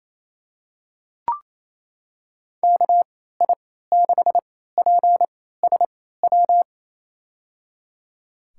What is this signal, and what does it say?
Morse code, heard as a keyed tone, about 27 wpm, spaced as for 16 wpm, 700 Hz. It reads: KI6PSW
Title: USA Call Signs - 6 Characters - 3x Character Spacing - 28wpm